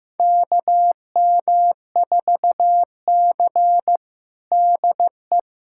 Morse code, heard as a keyed tone, about 15 wpm, 700 Hz. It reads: KM4C DE
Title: USA Callsign DE Callsign - 4 Characters - 15wpm